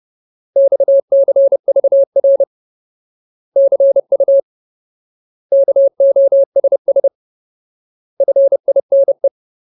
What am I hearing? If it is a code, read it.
Morse code, heard as a keyed tone, about 30 wpm, 550 Hz. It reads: XCVR CU KOSS FINE
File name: Ultimate Word List / 30WPM Ultimate